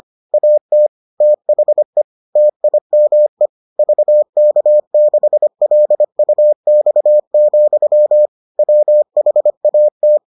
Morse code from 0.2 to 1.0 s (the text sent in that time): AT